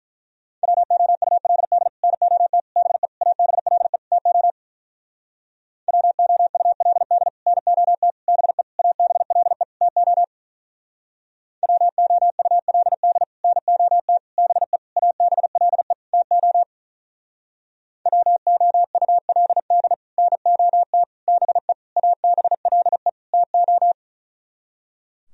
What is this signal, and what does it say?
Morse code, heard as a keyed tone, about 46 wpm, 700 Hz. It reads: WOULD NOT BE ABLE TO WOULD NOT BE ABLE TO WOULD NOT BE ABLE TO WOULD NOT BE ABLE TO